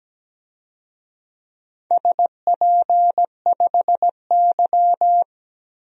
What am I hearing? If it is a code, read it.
Morse code, heard as a keyed tone, about 17 wpm, 700 Hz. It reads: SP5Y
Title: DXCC-100 Call Signs - 4 Characters - 17wpm